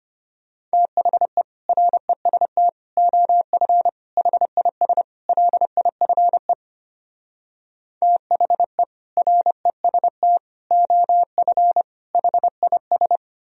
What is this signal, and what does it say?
Morse code, heard as a keyed tone, about 33 wpm, 700 Hz. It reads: THE REST OF HIS LIFE THE REST OF HIS